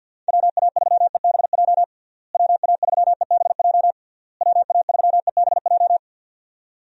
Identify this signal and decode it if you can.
Morse code, heard as a keyed tone, about 52 wpm, 700 Hz. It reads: WA3EBJ WA3EBJ WA3EBJ
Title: USA Call Signs - 6 Characters - Repeated 3x - 50wpm